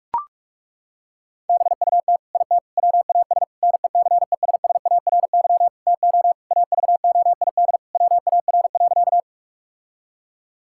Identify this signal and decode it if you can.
Morse code, heard as a keyed tone, about 42 wpm, 700 Hz. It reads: BUT IT WAS NECESSARY TO AVOID WAR.
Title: Sentences from Top 1000 Words - Review Part 6 - 45wpm